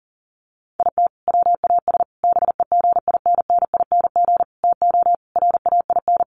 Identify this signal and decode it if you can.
Morse code, heard as a keyed tone, about 40 wpm, 700 Hz. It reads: IT WAS BEGINNING TO RAIN
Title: Sentences from Top 800 Words 40wpm